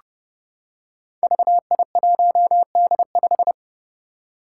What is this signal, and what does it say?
Morse code, heard as a keyed tone, about 30 wpm, 700 Hz. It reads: VI1D5
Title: Five Challenging Letter-Number Combinations 30wpm